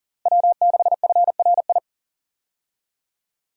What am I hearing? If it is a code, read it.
Morse code, heard as a keyed tone, about 40 wpm, 700 Hz. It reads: W6FRI
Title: USA Call Signs - 5 Characters - Repeated 3x - 40wpm